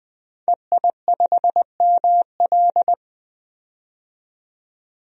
Morse code, heard as a keyed tone, about 21 wpm, 700 Hz. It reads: EI5ML
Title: DXCC-100 Call Signs - 5 Characters - 20wpm